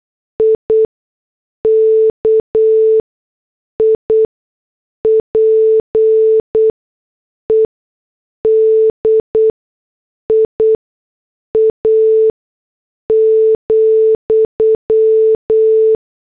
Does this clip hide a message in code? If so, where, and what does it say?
Yes: Morse code from 0.4 to 15.9 s, heard as a keyed tone, about 8 wpm, spaced as for 6 wpm, 440 Hz. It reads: IKIPEDIA,